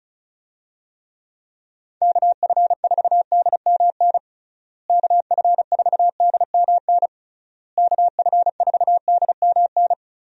Morse code, heard as a keyed tone, about 36 wpm, 700 Hz. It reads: KF4DMN KF4DMN KF4DMN
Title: USA Call Signs - 6 Characters - Repeated 3x - 35wpm